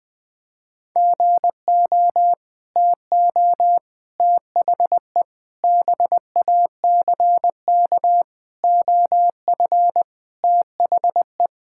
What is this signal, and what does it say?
Morse code, heard as a keyed tone, about 21 wpm, 700 Hz. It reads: GO TO THE BACK OF THE
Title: Sentences with 5-Word Phrases - 20wpm